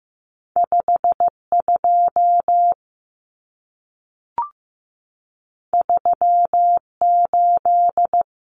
Morse code, heard as a keyed tone, about 15 wpm, 700 Hz. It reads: 52 38
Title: Numbers - 2-Digit 15wpm